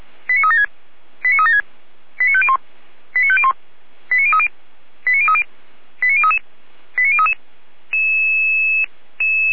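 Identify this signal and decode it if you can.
Five-tone selective call (ZVEI), heard as quick runs of five tones, tones of about 70 ms. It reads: 89376 89376 89501 89501 89030 89030 89033 89033
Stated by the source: ZVEI five-tone selective call received by SDR, fire brigade (Feuerwehr) Orlamünde.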